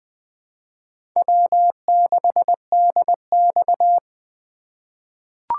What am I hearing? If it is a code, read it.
Morse code, heard as a keyed tone, about 21 wpm, 700 Hz. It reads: W6DX
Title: USA Call Signs - 4 Characters - 3x Character Spacing - 20wpm